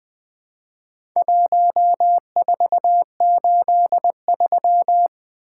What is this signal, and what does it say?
Morse code, heard as a keyed tone, about 21 wpm, 700 Hz. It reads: 1483
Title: Four Letter-Number Combinations - 20wpm